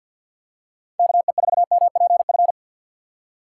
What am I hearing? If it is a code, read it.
Morse code, heard as a keyed tone, about 53 wpm, 700 Hz. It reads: KE4MPF
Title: USA Call Signs - 6 Characters - 3x Character Spacing - 50wpm